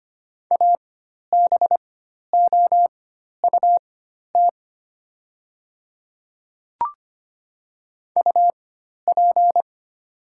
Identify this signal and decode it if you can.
Morse code, heard as a keyed tone, about 25 wpm, spaced as for 12 wpm, 700 Hz. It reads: ABOUT UP
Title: 4x Character Spacing - Top 100 Words - 25wpm